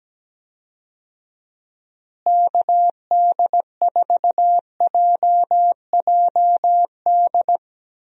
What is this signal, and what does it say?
Morse code, heard as a keyed tone, about 17 wpm, 700 Hz. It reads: KD4JJD
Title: USA Call Signs - 6 Characters - Repeated 2x - 17wpm